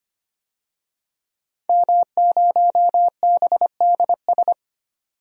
Morse code, heard as a keyed tone, about 25 wpm, 700 Hz. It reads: M0BDS
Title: DXCC-100 Call Signs - 5 Characters - 25wpm